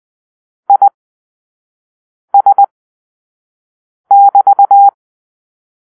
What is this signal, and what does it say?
Morse code, heard as a keyed tone, about 20 wpm, spaced as for 5 wpm, 800 Hz. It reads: IS<BT>
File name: arrl / Grp-6 Drill-4